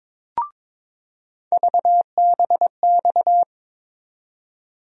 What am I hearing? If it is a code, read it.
Morse code, heard as a keyed tone, about 21 wpm, 700 Hz. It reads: VBX
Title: USA Call Sign Postfixes - 3 Letters 22wpm